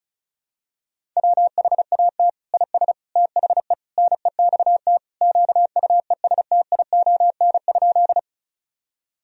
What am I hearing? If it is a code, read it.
Morse code, heard as a keyed tone, about 36 wpm, 700 Hz. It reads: WHAT IS THE NEXT QUESTION?
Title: Sentences from Top 1000 Words - Review Part 1 - 35wpm